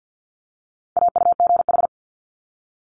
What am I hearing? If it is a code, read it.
Morse code, heard as a keyed tone, about 50 wpm, 700 Hz. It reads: AUGH
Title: Multi-letter Phonemes 50wpm